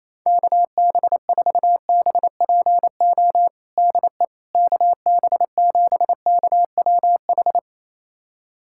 Morse code, heard as a keyed tone, about 28 wpm, 700 Hz. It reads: KB4BPO DE KB7KWH
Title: USA Callsign DE Callsign - 6 Characters - 28wpm